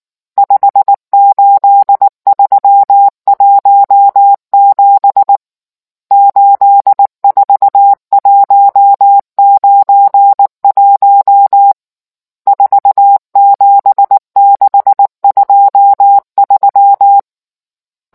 Morse code, heard as a keyed tone, about 19 wpm, 800 Hz. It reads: 58317 84191 47623